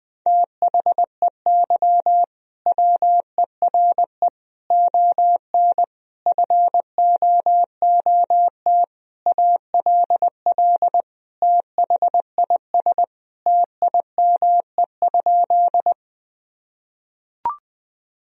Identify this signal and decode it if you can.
Morse code, heard as a keyed tone, about 20 wpm, 700 Hz. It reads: THEY WERE ON FOOT ALL THIS TIME?
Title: Sentences from Top 1000 Words - Review Part 3 - 20wpm